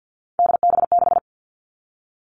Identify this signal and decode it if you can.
Morse code, heard as a keyed tone, about 50 wpm, 700 Hz. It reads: DB6
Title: Three Challenging Letter-Number Combinations 50wpm